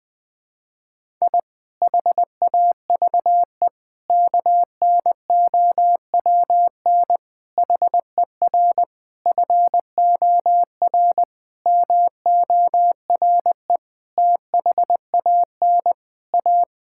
Morse code, heard as a keyed tone, about 20 wpm, 700 Hz. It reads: I HAVE KNOWN HER FOR MORE THAN A
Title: Sentences with 5-Word Phrases - 20wpm